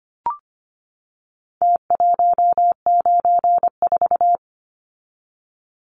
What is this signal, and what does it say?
Morse code, heard as a keyed tone, about 25 wpm, 700 Hz. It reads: T194
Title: Numbers - Contest Style - 4-Digit 25wpm